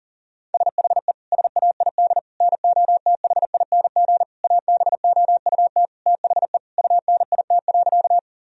Morse code, heard as a keyed tone, about 40 wpm, 700 Hz. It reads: SHE SAID NOTHING ABOUT THE UNIT.
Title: Sentences from Top 1000 Words - Review Part 2 - 40wpm